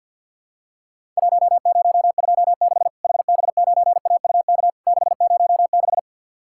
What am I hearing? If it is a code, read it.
Morse code, heard as a keyed tone, about 50 wpm, 700 Hz. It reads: 1026 HB9AUK 606
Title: Worked All Europe DX Contest - QTC Exchange - 50wpm